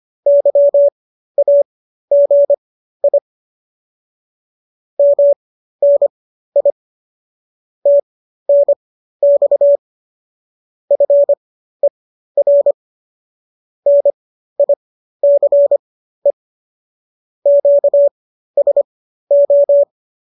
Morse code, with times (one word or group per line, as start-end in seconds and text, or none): 0.3-3.2 YAGI
5.0-6.7 MNI
7.9-9.8 TNX
10.9-12.7 FER
13.9-16.3 NICE
17.5-19.8 QSO